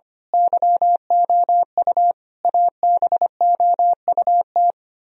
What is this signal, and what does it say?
Morse code, heard as a keyed tone, about 26 wpm, 700 Hz. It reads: YOU ABOUT
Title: Speed-Racing - Sets of 5 Words 25wpm